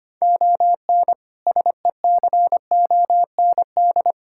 Morse code, heard as a keyed tone, about 25 wpm, 700 Hz. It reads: ON SECOND
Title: Sentences from Top 1600 Words 25wpm